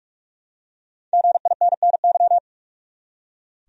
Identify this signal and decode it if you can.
Morse code, heard as a keyed tone, about 43 wpm, 700 Hz. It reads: GINNY
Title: USA Common Names - Max 5 Letters - 45wpm